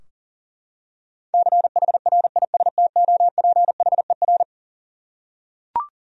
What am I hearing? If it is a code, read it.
Morse code, heard as a keyed tone, about 37 wpm, 700 Hz. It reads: CHRISTOPHER
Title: USA Common Names - 40wpm